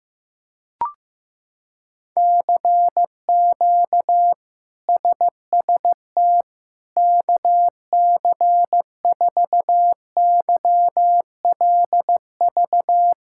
Morse code, heard as a keyed tone, about 16 wpm, 700 Hz. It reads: CQ SST KC4YLV
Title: K1USN SST QSO Practice - 15wpm